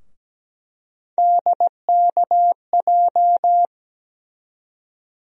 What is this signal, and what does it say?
Morse code, heard as a keyed tone, about 17 wpm, 700 Hz. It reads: DKJ
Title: USA Call Sign Postfixes - 3 Letters 17wpm